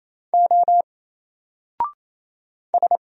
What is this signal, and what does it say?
Morse code, heard as a keyed tone, about 28 wpm, 700 Hz. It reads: O S
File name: Single-Letter-Number-28wpm